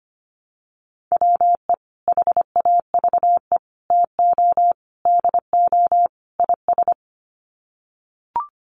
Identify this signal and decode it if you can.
Morse code, heard as a keyed tone, about 25 wpm, 700 Hz. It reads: WE HAVE TO DO IS
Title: Sets of 5 Words Encore 25wpm